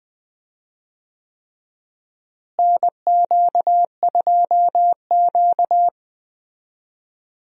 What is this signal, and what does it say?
Morse code, heard as a keyed tone, about 20 wpm, 700 Hz. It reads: NQ2Q
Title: USA Call Signs - 4 Characters - Repeated 3x - 20wpm